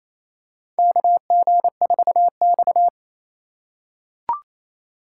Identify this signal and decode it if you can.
Morse code, heard as a keyed tone, about 27 wpm, 700 Hz. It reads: KG4X
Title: USA Call Signs - 2x Character Spacing - 28wpm